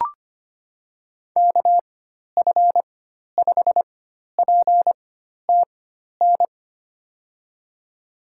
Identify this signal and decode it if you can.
Morse code, heard as a keyed tone, about 25 wpm, spaced as for 12 wpm, 700 Hz. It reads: KF5PTN
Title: USA Call Signs - 6 Characters - 4x Character Spacing - 25wpm